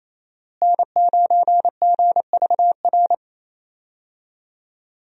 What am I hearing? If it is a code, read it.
Morse code, heard as a keyed tone, about 28 wpm, 700 Hz. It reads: N9GVR
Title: USA Call Signs - 5 Characters - Repeated 2x - 28wpm